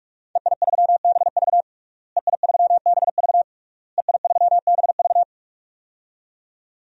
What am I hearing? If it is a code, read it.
Morse code, heard as a keyed tone, about 46 wpm, 700 Hz. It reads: EI3BV EI3BV EI3BV
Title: DXCC-100 Call Signs - 5 Characters - Repeated 3x - 45wpm